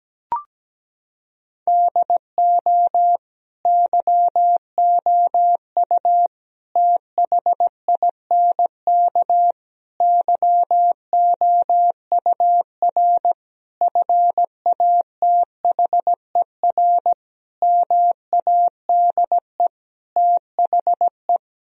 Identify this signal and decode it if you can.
Morse code, heard as a keyed tone, about 17 wpm, 700 Hz. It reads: DO YOU THINK YOUR FATHER MADE THE
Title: Sentences from Top 1000 Words - Review Part 1 - 17wpm